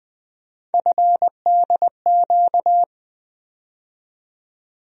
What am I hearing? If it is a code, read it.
Morse code, heard as a keyed tone, about 20 wpm, 700 Hz. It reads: FDQ
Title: USA Call Sign Postfixes - 3 Letters 20wpm